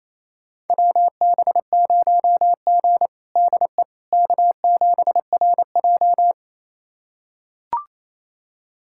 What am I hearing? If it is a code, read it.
Morse code, heard as a keyed tone, about 28 wpm, 700 Hz. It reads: WB0G DE K7RJ
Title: USA Callsign DE Callsign - 4 Characters - 28wpm